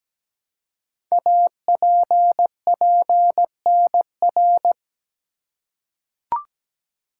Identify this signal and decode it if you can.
Morse code, heard as a keyed tone, about 17 wpm, 700 Hz. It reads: APPNR
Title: Code Groups - 5 Letter:Numbers - 17wpm